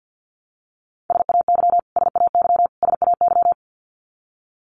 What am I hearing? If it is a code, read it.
Morse code, heard as a keyed tone, about 50 wpm, 700 Hz. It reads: SAY SAY SAY
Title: Top 500 Words Repeated 50wpm